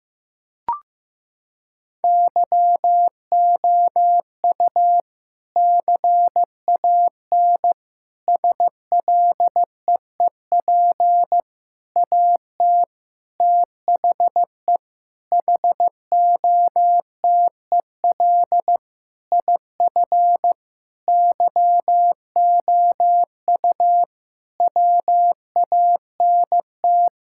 Morse code, heard as a keyed tone, about 15 wpm, 700 Hz. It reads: YOU CAN SLEEP AT THE HOTEL IF YOU WANT
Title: Sentences from Top 1600 Words 15wpm